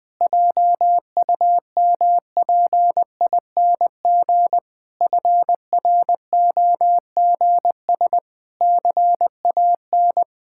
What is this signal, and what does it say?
Morse code, heard as a keyed tone, about 20 wpm, 700 Hz. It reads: JUMPING FROGS CAN
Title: Pangram Challenge v2 - 20wpm